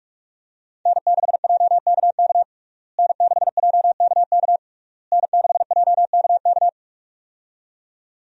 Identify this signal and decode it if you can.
Morse code, heard as a keyed tone, about 41 wpm, 700 Hz. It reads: N6JKK N6JKK N6JKK
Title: USA Call Signs - 5 Characters - Repeated 3x - 45wpm